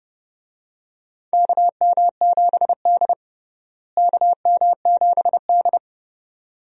Morse code, heard as a keyed tone, about 30 wpm, 700 Hz. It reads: KM7D KM7D
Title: USA Call Signs - 4 Characters - Repeated 2x - 30wpm